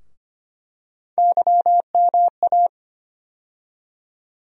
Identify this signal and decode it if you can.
Morse code, heard as a keyed tone, about 25 wpm, 700 Hz. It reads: YMA
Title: USA Call Sign Postfixes - 3 Letters 25wpm